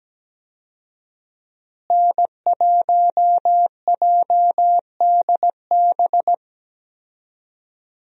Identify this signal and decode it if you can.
Morse code, heard as a keyed tone, about 17 wpm, 700 Hz. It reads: N1JDB
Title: USA Call Signs - 5 Characters - Repeated 2x - 17wpm